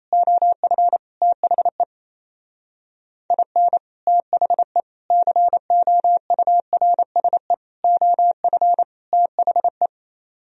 Morse code, heard as a keyed tone, about 38 wpm, 700 Hz. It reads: OF THE IN THE COURSE OF THE